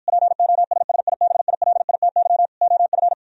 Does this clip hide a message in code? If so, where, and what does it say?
Yes: Morse code from 0.1 to 3.1 s, heard as a keyed tone, about 52 wpm, 700 Hz. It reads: POSSIBILITY OF